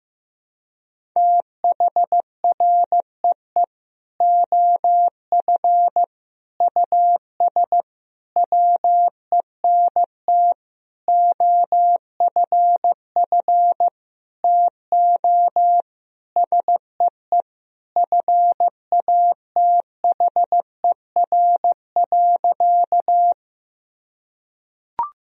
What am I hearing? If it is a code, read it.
Morse code, heard as a keyed tone, about 15 wpm, 700 Hz. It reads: THREE OF US WENT OFF TO SEE FATHER.
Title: Sentences from Top 1000 Words - Review Part 1 - 15wpm